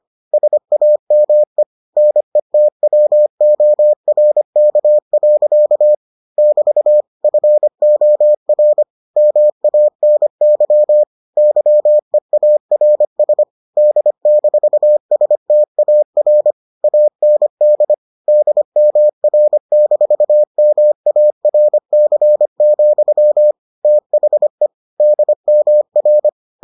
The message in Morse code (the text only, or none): SAME NETWORK. <BT> FOR MANY YEARS D-STAR AND DMR-MARC, THE DMR